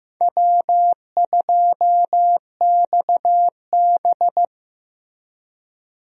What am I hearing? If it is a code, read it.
Morse code, heard as a keyed tone, about 15 wpm, 700 Hz. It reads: W2XB
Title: USA Call Signs - 4 Characters - Repeated 3x - 15wpm